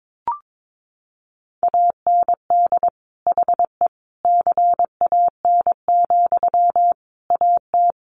Morse code, heard as a keyed tone, about 22 wpm, 700 Hz. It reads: AND HE CAN, AT
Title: Sentences from Top 900 Words 22wpm